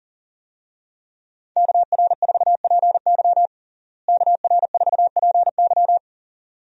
Morse code, heard as a keyed tone, about 37 wpm, 700 Hz. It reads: KR4PY KR4PY
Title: USA Call Signs - 5 Characters - Repeated 2x - 40wpm